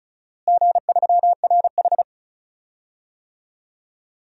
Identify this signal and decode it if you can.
Morse code, heard as a keyed tone, about 36 wpm, 700 Hz. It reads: G3RH
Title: DXCC-100 Call Signs - 4 Characters - Repeated 3x - 35wpm